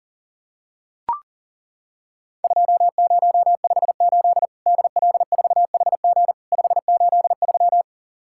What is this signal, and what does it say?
Morse code, heard as a keyed tone, about 40 wpm, 700 Hz. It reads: 2058 DL4HG 583
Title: Worked All Europe DX Contest - QTC Exchange - 40wpm